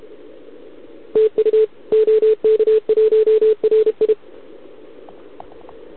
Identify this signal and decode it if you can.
Morse code, heard as a keyed tone, about 32 wpm, 440 Hz. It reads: TU OK1RI
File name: ARCHIV / OKDX2020 / OK1RIb